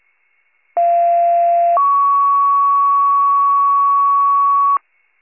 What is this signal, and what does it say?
Two-tone page, heard as a one-second tone, then a three-second tone, about 690 then 1120 Hz, with noise. It is heard from 0.8 to 4.8 s.
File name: pager / 46 RESCUE / wcalert